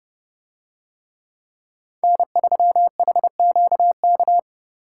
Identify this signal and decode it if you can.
Morse code, heard as a keyed tone, about 29 wpm, 700 Hz. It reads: N3HQK